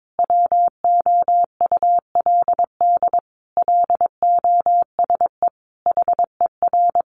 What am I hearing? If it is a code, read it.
Morse code, heard as a keyed tone, about 22 wpm, 700 Hz. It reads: WOULD LOSE HER
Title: Sentences from Top 1400 Words 22wpm